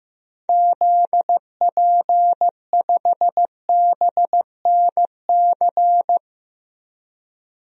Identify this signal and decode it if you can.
Morse code, heard as a keyed tone, about 15 wpm, 700 Hz. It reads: ZP5BNC
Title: DXCC-100 Call Signs - 6 Characters - 15wpm